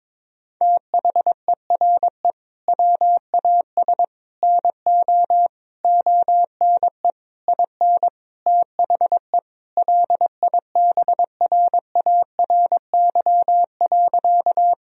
Morse code, heard as a keyed tone, about 22 wpm, 700 Hz. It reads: THERE WAS NO ONE IN THE LIBRARY.
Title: Sentences from Top 1000 Words - Review Part 4 - 22wpm